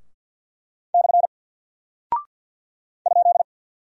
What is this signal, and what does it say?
Morse code, heard as a keyed tone, about 48 wpm, 700 Hz. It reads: / ?
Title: Single Character (Letter:Number:Punctuation:Prosign) Alt 50wpm